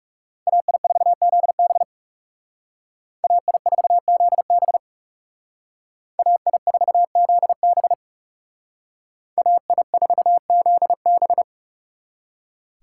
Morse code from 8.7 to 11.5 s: AI4ZB

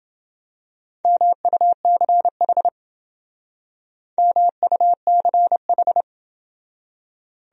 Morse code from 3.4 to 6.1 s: MUCH

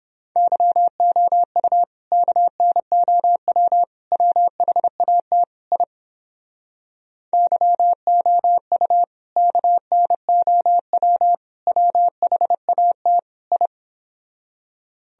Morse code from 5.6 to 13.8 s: I YOU KNOW WHAT I

